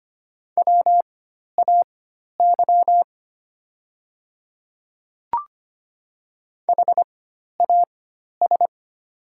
Morse code, heard as a keyed tone, about 25 wpm, spaced as for 12 wpm, 700 Hz. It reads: WAY HAS
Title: 4x Character Spacing - Top 100 Words - 25wpm